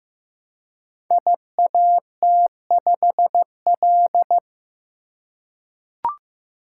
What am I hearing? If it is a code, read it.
Morse code, heard as a keyed tone, about 15 wpm, 700 Hz. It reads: IAT5L
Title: Code Groups - 5 Letter:Numbers - 15wpm